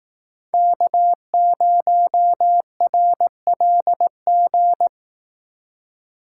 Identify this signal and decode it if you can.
Morse code, heard as a keyed tone, about 18 wpm, 700 Hz. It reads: K0RLG
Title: Speed Racing - USA 1x3 Call Signs 15wpm